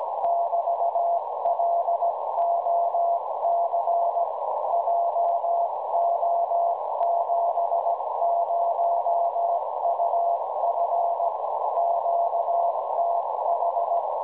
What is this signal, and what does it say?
Morse code, heard as a keyed tone, about 17 wpm, 690 Hz, with radio noise. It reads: <BT>LOC JO70UF QTH
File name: WebSDR records / OK0EV websdr recording 1854.0kHz